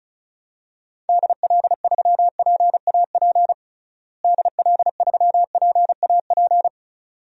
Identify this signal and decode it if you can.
Morse code, heard as a keyed tone, about 34 wpm, 700 Hz. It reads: DL3PAP DL3PAP
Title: DXCC-100 Call Signs - 6 Characters - Repeated 2x - 35wpm